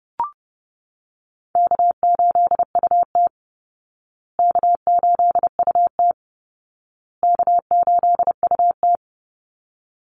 Morse code, heard as a keyed tone, about 30 wpm, 700 Hz. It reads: K8UT K8UT K8UT
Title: Easy Call Signs Repeated 3x - 30wpm